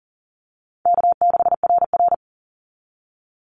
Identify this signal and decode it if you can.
Morse code, heard as a keyed tone, about 40 wpm, 700 Hz. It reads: K6RR
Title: Easy Call Signs Repeated 2x - 40wpm